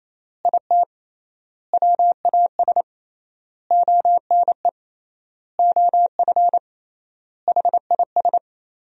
Morse code, heard as a keyed tone, about 28 wpm, 700 Hz. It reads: IT WAS ONE OF HIS
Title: 3x Word-Spacing - Sentences from Top 300 Words - 28wpm